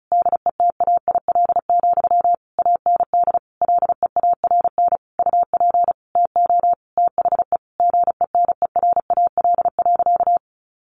Morse code, heard as a keyed tone, about 34 wpm, 700 Hz. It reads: DETAIL, AND LEARN UP TO THE GENERAL.
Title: Sentences from Top 1500 Words 35wpm